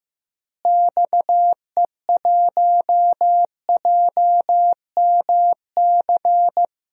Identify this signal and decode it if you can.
Morse code, heard as a keyed tone, about 15 wpm, 700 Hz. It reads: XE1JMC
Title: DXCC-100 Call Signs - 6 Characters - Repeated 2x - 15wpm